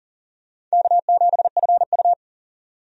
Morse code, heard as a keyed tone, about 40 wpm, 700 Hz. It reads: K7FU
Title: USA Call Signs - 4 Characters - 40wpm